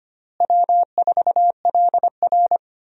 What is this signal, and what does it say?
Morse code, heard as a keyed tone, about 25 wpm, 700 Hz. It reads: W4LR
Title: USA Call Signs - 4 Characters - Repeated 2x - 25wpm